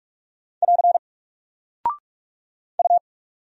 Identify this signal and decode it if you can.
Morse code, heard as a keyed tone, about 44 wpm, 700 Hz. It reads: <AR> U
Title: Single Character (Letter:Number:Punctuation:Prosign) 45wpm